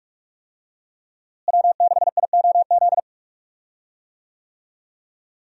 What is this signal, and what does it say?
Morse code, heard as a keyed tone, about 45 wpm, 700 Hz. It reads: W6IOZ
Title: USA Call Signs - 5 Characters - Repeated 2x - 45wpm